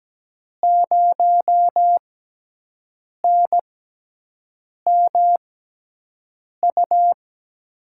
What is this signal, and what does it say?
Morse code, heard as a keyed tone, about 17 wpm, spaced as for 6 wpm, 700 Hz. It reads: 0NMU
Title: USA Call Signs - 5 Characters - 6x Character Spacing - 17wpm